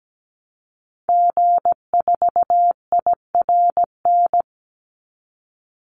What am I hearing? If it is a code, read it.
Morse code, heard as a keyed tone, about 17 wpm, 700 Hz. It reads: G4IRN
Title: Call Signs Repeated 3x - 17wpm